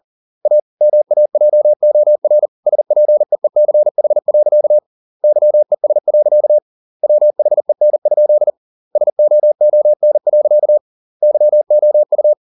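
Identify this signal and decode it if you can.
Morse code, heard as a keyed tone, about 40 wpm, 600 Hz. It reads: A MAJOR SPEECH. YES. WHEN? SOON. YOU
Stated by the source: Morse code at 40 wpm, practice text, CW med Gnister.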